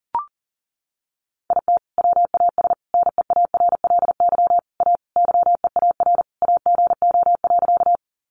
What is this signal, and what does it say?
Morse code, heard as a keyed tone, about 40 wpm, 700 Hz. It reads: IT WAS NEARLY A YEAR AGO.
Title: Sentences from Top 1300 Words 40wpm